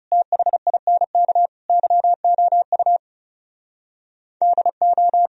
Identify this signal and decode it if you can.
Morse code, heard as a keyed tone, about 37 wpm, 700 Hz. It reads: THINK YOU DO
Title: Speed-Racing - Sets of 4 Words 30wpm